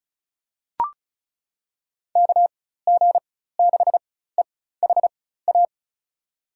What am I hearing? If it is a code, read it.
Morse code, heard as a keyed tone, about 36 wpm, spaced as for 16 wpm, 700 Hz. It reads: KG6EHA
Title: USA Call Signs - 6 Characters - 4x Character Spacing - 35wpm